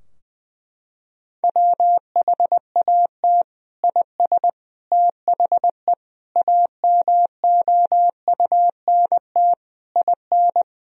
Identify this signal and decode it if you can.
Morse code, heard as a keyed tone, about 21 wpm, 700 Hz. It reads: WHAT IS THE AMOUNT IN
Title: Sentences from Top 1000 Words - Review Part 4 - 20wpm